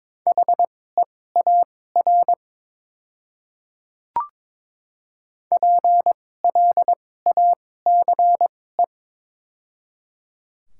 Morse code, heard as a keyed tone, about 22 wpm, spaced as for 16 wpm, 700 Hz. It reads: HEAR PLACE
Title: 2x Character Spacing - Top 500 Words - 22wpm